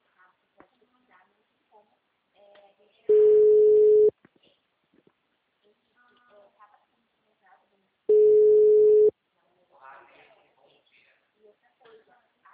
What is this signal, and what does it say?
Ringback tone (Europe, 425 Hz), with speech.
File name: out- -2005- - - .8